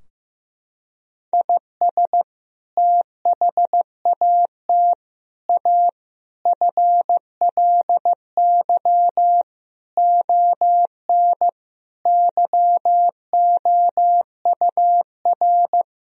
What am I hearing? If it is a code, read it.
Morse code, heard as a keyed tone, about 15 wpm, 700 Hz. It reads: IS THAT A FLY ON YOUR
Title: Sentences from Top 1000 Words - Review Part 2 - 15wpm